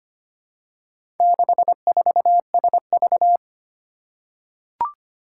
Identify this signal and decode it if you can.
Morse code, heard as a keyed tone, about 25 wpm, 700 Hz. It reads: 64SV
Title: Four Challenging Letter-Number Combinations 25wpm